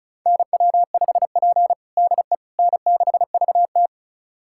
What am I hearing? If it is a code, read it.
Morse code, heard as a keyed tone, about 36 wpm, 700 Hz. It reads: NW5P DE N6VT
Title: USA Callsign DE Callsign - 4 Characters - 35wpm